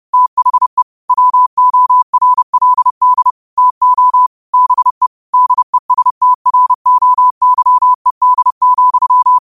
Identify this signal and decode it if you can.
Morse code, heard as a keyed tone, about 30 wpm, 1000 Hz. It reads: THE WORLD TO BE DESTROYED,
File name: qotdfiles / 30wpm